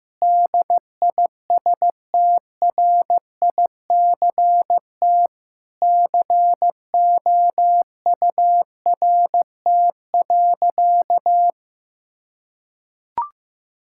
Morse code, heard as a keyed tone, about 16 wpm, 700 Hz. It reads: DISTRICT COURT.
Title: Sentences from Top 1600 Words 15wpm